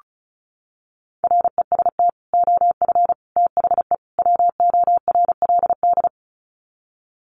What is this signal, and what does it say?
Morse code, heard as a keyed tone, about 34 wpm, 700 Hz. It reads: REST OF THE WORLD
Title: Sets of 4 Words 35wpm